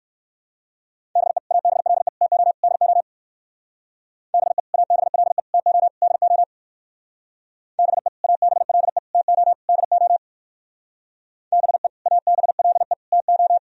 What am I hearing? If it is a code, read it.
Morse code, heard as a keyed tone, about 58 wpm, 700 Hz. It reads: BE ABLE TO DO BE ABLE TO DO BE ABLE TO DO BE ABLE TO